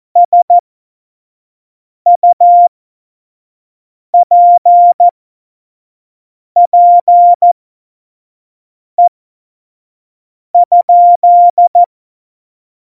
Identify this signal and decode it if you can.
Morse code, heard as a keyed tone, about 14 wpm, spaced as for 5 wpm, 700 Hz. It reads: SUPPE?
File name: Morsekurs / Witze / Witz 26. November 2019, 25 F70